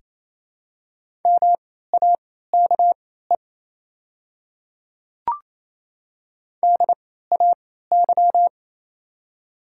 Morse code, heard as a keyed tone, about 27 wpm, spaced as for 16 wpm, 700 Hz. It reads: MAKE DAY
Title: 3x Character Spacing - Top 100 Words - 28wpm